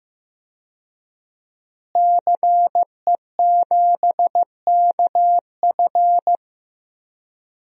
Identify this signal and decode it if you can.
Morse code, heard as a keyed tone, about 15 wpm, 700 Hz. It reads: CE7KF
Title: DXCC-100 Call Signs - 5 Characters - Repeated 3x - 15wpm